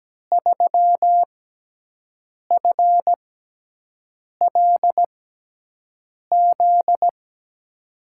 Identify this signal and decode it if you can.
Morse code, heard as a keyed tone, about 17 wpm, spaced as for 6 wpm, 700 Hz. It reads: 3FLZ
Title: USA Call Signs - 6x Character Spacing - 17wpm